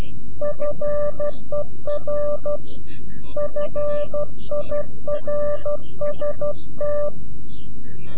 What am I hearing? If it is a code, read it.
Morse code, heard as a keyed tone, about 12 wpm, 590 Hz. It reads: FER FIRST
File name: CW-QSO / CQ Call / CQ-QSO